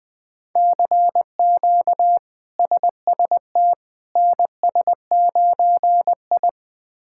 Morse code, heard as a keyed tone, about 20 wpm, 700 Hz. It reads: CQ SST NS9I
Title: K1USN SST QSO Practice - 20wpm